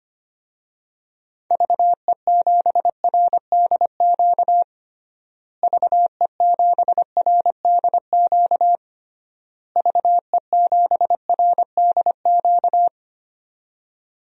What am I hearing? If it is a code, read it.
Morse code, heard as a keyed tone, about 25 wpm, 700 Hz. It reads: VE7RDQ VE7RDQ VE7RDQ
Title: DXCC-100 Call Signs - 6 Characters - Repeated 3x - 25wpm